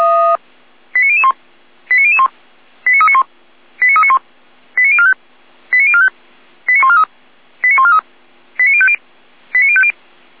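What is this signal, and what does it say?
Five-tone selective call (ZVEI), heard as quick runs of five tones, tones of about 80 ms. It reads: 89001 89001 89391 89391 89045 89045 89143 89143 89050 89050
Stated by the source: ZVEI five-tone selective call received on an SDR.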